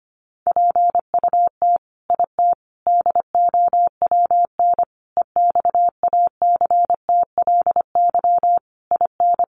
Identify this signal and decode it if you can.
Morse code, heard as a keyed tone, about 25 wpm, 700 Hz. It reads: PUT IT DOWN EXACTLY IN
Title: Sentences from Top 900 Words 25wpm